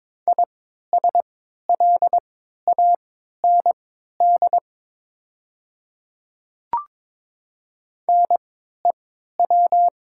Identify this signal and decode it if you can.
Morse code, heard as a keyed tone, about 22 wpm, spaced as for 12 wpm, 700 Hz. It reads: ISLAND NEW
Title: 3x Character Spacing - Top 500 Words - 22wpm